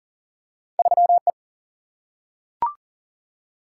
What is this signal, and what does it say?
Morse code, heard as a keyed tone, about 40 wpm, 700 Hz. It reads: 3 E
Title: Single-Letter Single-Number 40wpm